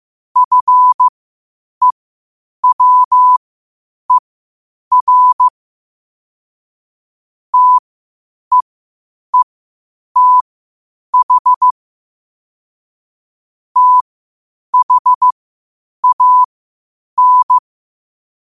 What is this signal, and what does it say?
Morse code, heard as a keyed tone, about 15 wpm, spaced as for 8 wpm, 1000 Hz. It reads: FEWER TEETH THAN
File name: qotdfiles / 8wpm